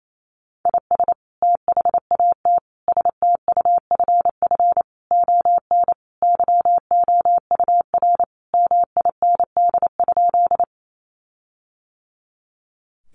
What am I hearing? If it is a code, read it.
Morse code, heard as a keyed tone, about 28 wpm, 700 Hz. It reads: IS THAT STUFF ON YOUR MIND?
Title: Sentences from Top 1400 Words 28wpm